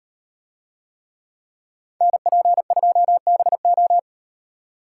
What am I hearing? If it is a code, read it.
Morse code, heard as a keyed tone, about 38 wpm, 700 Hz. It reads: NP2BO